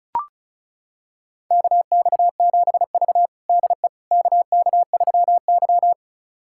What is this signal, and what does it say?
Morse code, heard as a keyed tone, about 34 wpm, 700 Hz. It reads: KX7V DE KK3Y
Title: USA Callsign DE Callsign - 4 Characters - 35wpm